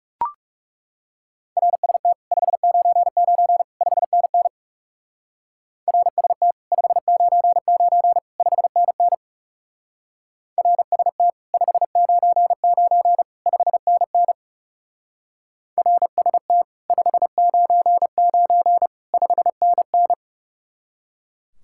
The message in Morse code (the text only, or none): RST 599 5NN RST 599 5NN RST 599 5NN RST 599 5NN